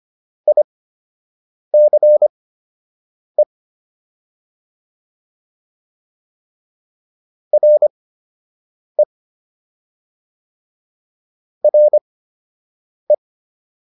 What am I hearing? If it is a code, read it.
Morse code, heard as a keyed tone, about 25 wpm, spaced as for 7 wpm, 600 Hz. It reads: ICE RE RE